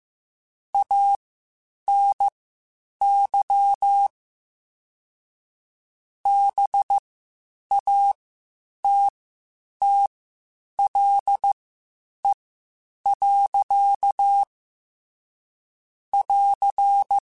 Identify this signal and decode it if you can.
Morse code, heard as a keyed tone, about 15 wpm, spaced as for 8 wpm, 800 Hz. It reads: ANY BATTLE. <AR>
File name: Quotes Spaced / Quotes Spaced-mp3s / Pessimism sp @13